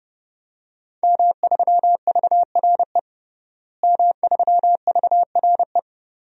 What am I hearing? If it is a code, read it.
Morse code, heard as a keyed tone, about 32 wpm, 700 Hz. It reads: M3VRE M3VRE
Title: DXCC-100 Call Signs - 5 Characters - Repeated 3x - 30wpm